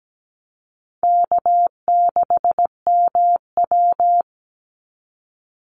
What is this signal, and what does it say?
Morse code, heard as a keyed tone, about 17 wpm, 700 Hz. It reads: K6MW
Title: NA SOTA Chaser Callsigns 17wpm